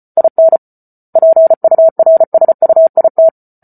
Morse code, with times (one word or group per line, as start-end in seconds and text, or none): 0.2-0.6 IN
1.2-3.3 PURSUIT